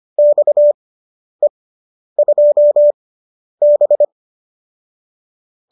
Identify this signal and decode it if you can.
Morse code, heard as a keyed tone, about 25 wpm, spaced as for 10 wpm, 590 Hz. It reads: XE2B